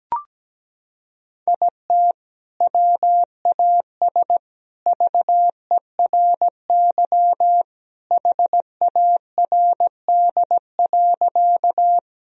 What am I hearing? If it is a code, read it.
Morse code, heard as a keyed tone, about 17 wpm, 700 Hz. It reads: IT WAS VERY HARD.
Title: Sentences from Top 1000 Words - Review Part 1 - 17wpm